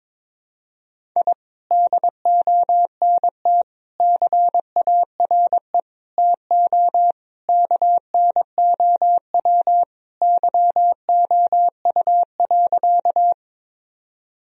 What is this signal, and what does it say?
Morse code, heard as a keyed tone, about 21 wpm, 700 Hz. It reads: I DONT CARE TO KNOW YOU.
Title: Sentences from Top 1000 Words - Review Part 1 - 22wpm